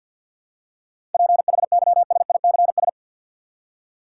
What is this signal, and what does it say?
Morse code, heard as a keyed tone, about 53 wpm, 700 Hz. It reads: PHYSICS
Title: Seven Letter Words - 50wpm